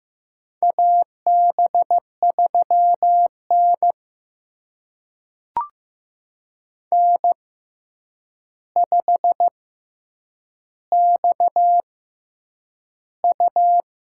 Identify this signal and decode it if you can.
Morse code, heard as a keyed tone, about 15 wpm, spaced as for 8 wpm, 700 Hz. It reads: AB3N N5XU
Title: USA Call Signs - 4 Characters - 6x Character Spacing - 15wpm